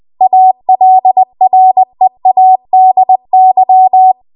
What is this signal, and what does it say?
Morse code, heard as a keyed tone, about 20 wpm, 750 Hz. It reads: ALREADY
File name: Morse / 20WPM